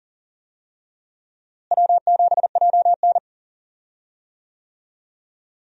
Morse code, heard as a keyed tone, about 40 wpm, 700 Hz. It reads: W7JN
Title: USA Call Signs - 4 Characters - Repeated 3x - 40wpm